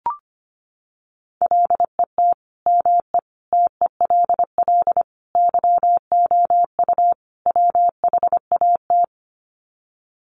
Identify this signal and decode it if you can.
Morse code, heard as a keyed tone, about 25 wpm, 700 Hz. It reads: LET ME TELL YOU WHAT
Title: Sets of 5 Words Encore 25wpm